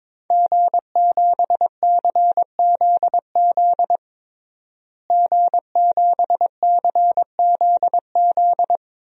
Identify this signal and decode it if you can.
Morse code, heard as a keyed tone, about 22 wpm, 700 Hz. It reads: G7CZZ G7CZZ
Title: DXCC-100 Call Signs - 5 Characters - Repeated 2x - 22wpm